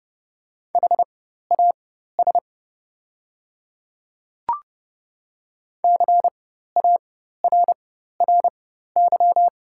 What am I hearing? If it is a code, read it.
Morse code, heard as a keyed tone, about 30 wpm, spaced as for 14 wpm, 700 Hz. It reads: HAS CARRY
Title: 4x Character Spacing - Top 300 Words - 30wpm